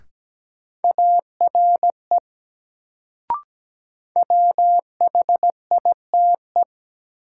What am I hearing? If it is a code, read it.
Morse code, heard as a keyed tone, about 17 wpm, 700 Hz. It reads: ARE WHITE
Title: Sets of 2 Words - New Word Drill - 17wpm